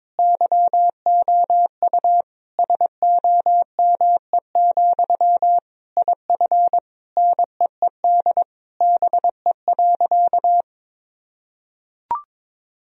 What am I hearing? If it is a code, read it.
Morse code, heard as a keyed tone, about 22 wpm, 700 Hz. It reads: YOU SOME, IF NEED BE.
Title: Sentences from Top 1000 Words - Review Part 1 - 22wpm